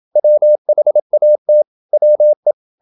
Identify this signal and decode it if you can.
Morse code, heard as a keyed tone, about 27 wpm, 600 Hz. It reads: WHAT WE
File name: Never 0027wpm